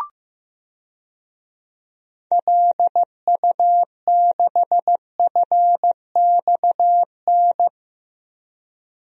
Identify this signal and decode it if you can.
Morse code, heard as a keyed tone, about 15 wpm, 700 Hz. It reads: LU6FXN
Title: DXCC-100 Call Signs - 6 Characters - Repeated 2x - 15wpm